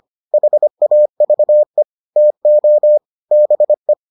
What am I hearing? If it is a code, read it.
Morse code, heard as a keyed tone, about 25 wpm, 600 Hz. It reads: HAVE TO BE